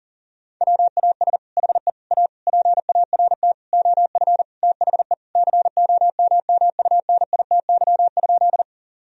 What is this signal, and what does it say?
Morse code, heard as a keyed tone, about 40 wpm, 700 Hz. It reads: WAS HE A PART OF THE COMMUNITY?
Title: Sentences from Top 1000 Words - Review Part 4 - 40wpm